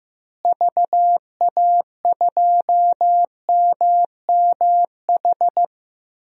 Morse code, heard as a keyed tone, about 15 wpm, 700 Hz. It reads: VA2MMH
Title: DXCC-100 Call Signs - 6 Characters - Repeated 2x - 15wpm